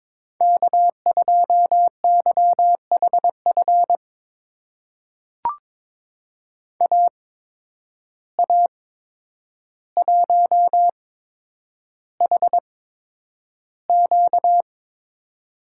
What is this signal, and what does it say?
Morse code, heard as a keyed tone, about 22 wpm, spaced as for 9 wpm, 700 Hz. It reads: K2YHF AA1HQ
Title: USA Call Signs - 5 Characters - 8x Character Spacing - 22wpm